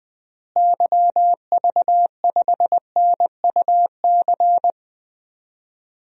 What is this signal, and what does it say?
Morse code, heard as a keyed tone, about 20 wpm, 700 Hz. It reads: YV5NUC
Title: DXCC-100 Call Signs - 6 Characters - Repeated 2x - 20wpm